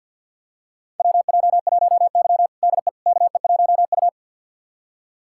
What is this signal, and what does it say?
Morse code, heard as a keyed tone, about 49 wpm, 700 Hz. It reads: WJ1Y DE KE1U
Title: USA Callsign DE Callsign - 4 Characters - 50wpm